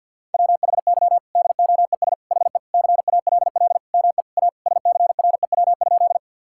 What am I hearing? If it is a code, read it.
Morse code, heard as a keyed tone, about 52 wpm, 700 Hz. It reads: WHY DOES HE CALL ME A SCREW?
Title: Sentences from Top 1600 Words 50wpm